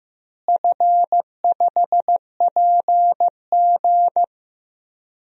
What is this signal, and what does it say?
Morse code, heard as a keyed tone, about 15 wpm, 700 Hz. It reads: F5PG
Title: DXCC-100 Call Signs - 4 Characters - Repeated 3x - 15wpm